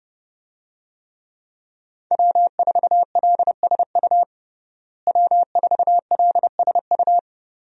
Morse code, heard as a keyed tone, about 30 wpm, 700 Hz. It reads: W4LSU W4LSU
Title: USA Call Signs - 5 Characters - Repeated 3x - 30wpm